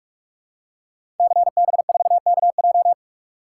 Morse code, heard as a keyed tone, about 44 wpm, 700 Hz. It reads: CB4KJ
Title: Code Groups - 5 Letter:Numbers - 45wpm